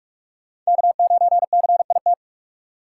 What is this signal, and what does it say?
Morse code, heard as a keyed tone, about 46 wpm, 700 Hz. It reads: K9CIT
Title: USA Call Signs - 5 Characters - 3x Character Spacing - 45wpm